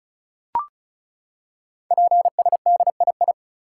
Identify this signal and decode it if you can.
Morse code, heard as a keyed tone, about 36 wpm, 700 Hz. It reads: PSDII
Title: Code Groups - 5 Letter:Numbers - 35wpm